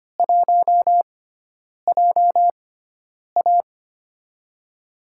Morse code, heard as a keyed tone, about 25 wpm, spaced as for 9 wpm, 700 Hz. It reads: 1JA
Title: USA Call Signs - 4 Characters - 6x Character Spacing - 25wpm